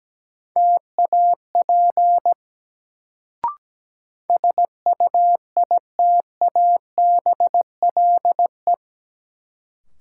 Morse code, heard as a keyed tone, about 17 wpm, 700 Hz. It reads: TAP SUITABLE
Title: Sentences from Top 1400 Words - New Word Drill Alt - 17wpm